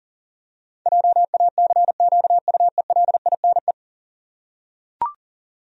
Morse code, heard as a keyed tone, about 40 wpm, 700 Hz. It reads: JACQUELINE
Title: USA Common Names - 40wpm